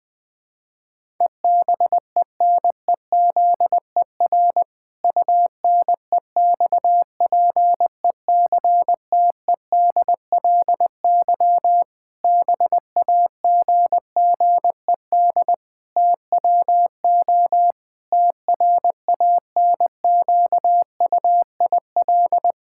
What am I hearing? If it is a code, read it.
Morse code, heard as a keyed tone, about 20 wpm, 700 Hz. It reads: EBENEZER UNEXPECTEDLY BAGGED TWO TRANQUIL
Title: Pangram Challenge v2 - 20wpm